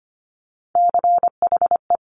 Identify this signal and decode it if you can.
Morse code, heard as a keyed tone, about 25 wpm, 700 Hz. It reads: CHE
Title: Top Three Letter Combinations in Words 25wpm